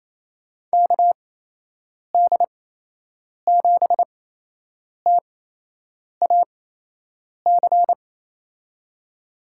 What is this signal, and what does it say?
Morse code, heard as a keyed tone, about 27 wpm, spaced as for 8 wpm, 700 Hz. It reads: KD7TAC
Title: USA Call Signs - 6 Characters - 8x Character Spacing - 28wpm